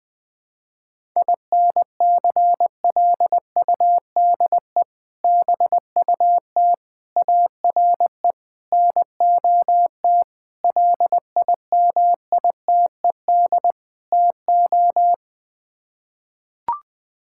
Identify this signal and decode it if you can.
Morse code, heard as a keyed tone, about 20 wpm, 700 Hz. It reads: INCLUDE BUT ARE NOT LIMITED TO
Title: Sets of 6 Words - 20wpm